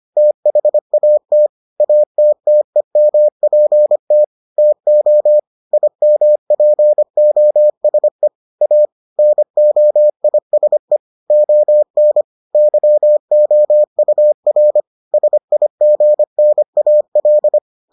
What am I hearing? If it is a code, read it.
Morse code, heard as a keyed tone, about 25 wpm, 600 Hz. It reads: THAT ATTEMPT TO IMPOSE A NOISE ON YOUR SIGNAL